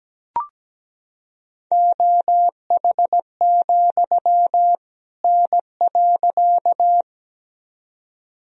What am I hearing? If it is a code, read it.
Morse code, heard as a keyed tone, about 17 wpm, 700 Hz. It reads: OH, N.
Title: Sentences from Top 500 Words and Abbreviations 17wpm